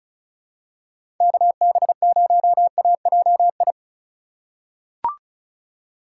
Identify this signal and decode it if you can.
Morse code, heard as a keyed tone, about 36 wpm, 700 Hz. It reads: KB0AJI
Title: USA Call Signs - 6 Characters - 2x Character Spacing - 35wpm